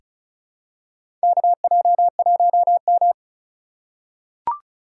Morse code, heard as a keyed tone, about 34 wpm, 700 Hz. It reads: KJ1M
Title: USA Call Signs - 4 Characters - 3x Character Spacing - 35wpm